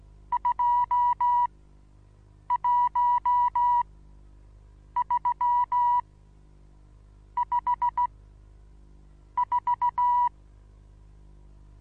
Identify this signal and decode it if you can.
Morse code, heard as a keyed tone, about 16 wpm, spaced as for 6 wpm, 970 Hz. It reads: 21354
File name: Onlinelehrgang / morsen / DJ4UF Morsen25